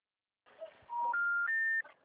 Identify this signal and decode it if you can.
Special information tone.
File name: out- -2021